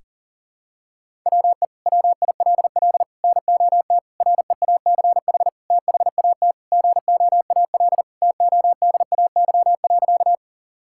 Morse code, heard as a keyed tone, about 37 wpm, 700 Hz. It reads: WE WILL NOT REACH THAT GOAL TODAY.
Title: Sentences from Top 1000 Words - Review Part 5 - 40wpm